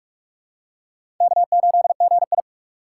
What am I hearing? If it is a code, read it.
Morse code, heard as a keyed tone, about 46 wpm, 700 Hz. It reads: K8GI
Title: USA Call Signs - 4 Characters - 6x Character Spacing - 45wpm